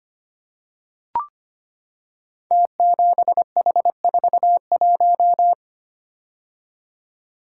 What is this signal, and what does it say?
Morse code, heard as a keyed tone, about 25 wpm, 700 Hz. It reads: T7H41
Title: Code Groups - 5 Letter:Numbers - 25wpm